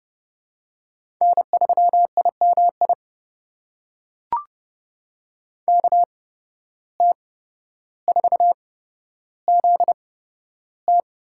Morse code, heard as a keyed tone, about 31 wpm, spaced as for 13 wpm, 700 Hz. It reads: N3IMI KT4ZT
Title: USA Call Signs - 5 Characters - 8x Character Spacing - 30wpm